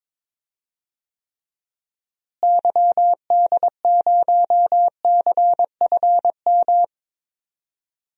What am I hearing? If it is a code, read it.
Morse code, heard as a keyed tone, about 22 wpm, 700 Hz. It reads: YD0CFM
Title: DXCC-100 Call Signs - 6 Characters - Repeated 3x - 22wpm